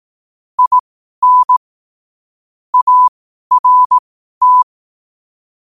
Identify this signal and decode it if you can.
Morse code, heard as a keyed tone, about 18 wpm, spaced as for 12 wpm, 1000 Hz. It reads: IN ART
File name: qotdfiles / 12wpm